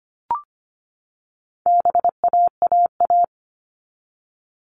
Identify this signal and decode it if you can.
Morse code, heard as a keyed tone, about 25 wpm, 700 Hz. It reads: BAAA
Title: All Cut Numbers - 4-Digit 25wpm